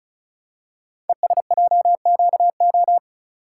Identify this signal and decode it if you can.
Morse code, heard as a keyed tone, about 36 wpm, 700 Hz. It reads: ESJQO
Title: Code Groups - 5 Letters - 35wpm